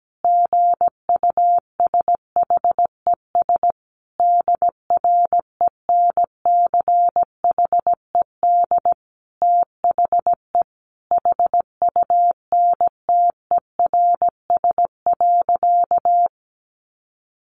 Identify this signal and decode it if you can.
Morse code, heard as a keyed tone, about 17 wpm, 700 Hz. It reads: GUSHES DRENCHED THE HUNTERS.
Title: A Sound Of Thunder 17wpm